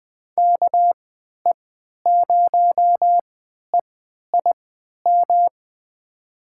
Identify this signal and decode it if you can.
Morse code, heard as a keyed tone, about 20 wpm, spaced as for 11 wpm, 700 Hz. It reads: KE0EIM
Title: USA Call Signs - 6 Characters - 3x Character Spacing - 20wpm